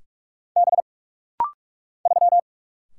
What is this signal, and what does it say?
Morse code, heard as a keyed tone, about 44 wpm, 700 Hz. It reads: B 3
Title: Single Letter-Number Alt - Rapid-Fire 45wpm